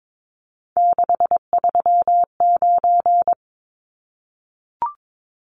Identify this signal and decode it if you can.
Morse code, heard as a keyed tone, about 22 wpm, 700 Hz. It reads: 639
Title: Numbers - 3-Digit 22wpm